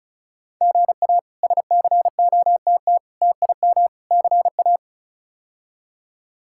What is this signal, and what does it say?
Morse code, heard as a keyed tone, about 34 wpm, 700 Hz. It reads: GA SCOTT TIM CA
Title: K1USN SST QSO Practice - 35wpm